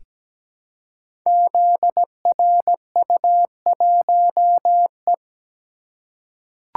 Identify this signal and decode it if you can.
Morse code, heard as a keyed tone, about 17 wpm, 700 Hz. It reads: ZRU1E
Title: Code Groups - 5 Letter:Numbers - 17wpm